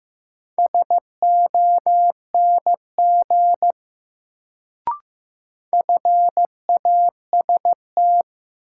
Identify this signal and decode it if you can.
Morse code, heard as a keyed tone, about 15 wpm, 700 Hz. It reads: SONG FAST
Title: Sentences from Top 400 Words - New Word Drill - 15wpm